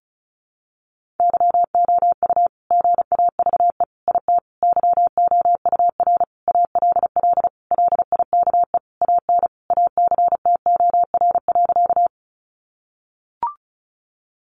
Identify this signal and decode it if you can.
Morse code, heard as a keyed tone, about 34 wpm, 700 Hz. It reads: YOU GAVE IT YOUR ALL LIKE AN ACTOR.
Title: Sentences from Top 1500 Words 35wpm